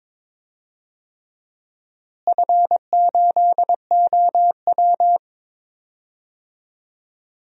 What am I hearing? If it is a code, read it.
Morse code, heard as a keyed tone, about 22 wpm, 700 Hz. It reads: F8OW
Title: DXCC-100 Call Signs - 4 Characters - Repeated 3x - 22wpm